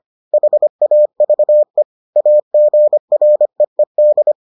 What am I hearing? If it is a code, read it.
Morse code, heard as a keyed tone, about 25 wpm, 600 Hz. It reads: HAVE AGREED